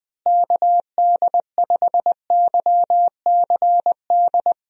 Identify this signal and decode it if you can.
Morse code, heard as a keyed tone, about 21 wpm, 700 Hz. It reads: KD5YCD
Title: USA Call Signs - 6 Characters - Repeated 2x - 20wpm